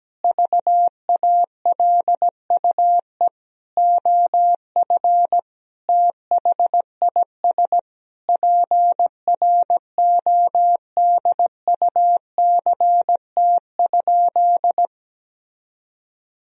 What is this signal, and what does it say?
Morse code, heard as a keyed tone, about 17 wpm, 700 Hz. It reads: VALUE OF THIS PRODUCT?
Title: Sentences from Top 1000 Words - Review Part 3 - 17wpm